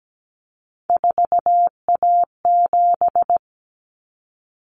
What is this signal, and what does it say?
Morse code, heard as a keyed tone, about 17 wpm, 700 Hz. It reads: 4A7
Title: Common Cut Numbers - 3-Digit 17wpm